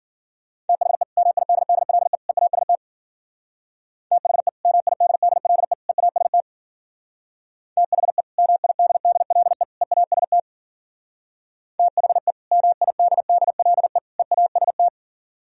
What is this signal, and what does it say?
Morse code, heard as a keyed tone, about 53 wpm, 700 Hz. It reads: THE MIDDLE EAST THE MIDDLE EAST THE MIDDLE EAST THE MIDDLE EAST